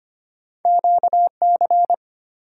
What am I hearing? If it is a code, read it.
Morse code, heard as a keyed tone, about 25 wpm, 700 Hz. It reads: QC
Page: morse-code-ninja-practice-25wpm